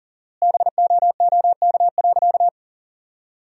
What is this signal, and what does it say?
Morse code, heard as a keyed tone, about 40 wpm, 700 Hz. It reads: BOOK.
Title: 4x Word-Spacing - Sentences from Top 500 Words - 40wpm